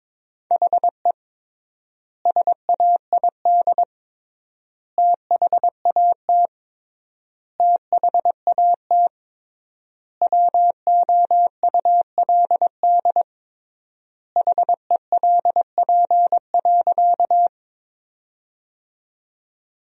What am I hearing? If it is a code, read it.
Morse code, heard as a keyed tone, about 22 wpm, 700 Hz. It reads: HE SAID THAT THAT WOULD HELP.
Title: 3x Word-Spacing - Sentences from Top 200 Words - 22wpm